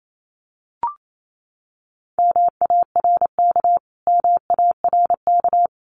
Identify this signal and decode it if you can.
Morse code, heard as a keyed tone, about 28 wpm, 700 Hz. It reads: MARK MARK
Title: Top 500 Words Repeated 28wpm